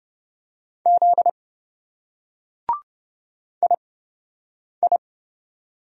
Morse code, heard as a keyed tone, about 28 wpm, spaced as for 7 wpm, 700 Hz. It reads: Z II